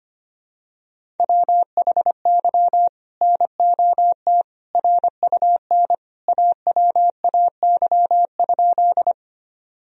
Morse code, heard as a keyed tone, about 25 wpm, 700 Hz. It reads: WHY NOT RUN AWAY?
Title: Sentences from Top 1000 Words - Review Part 1 - 25wpm